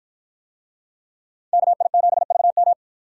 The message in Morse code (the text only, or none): KI7VM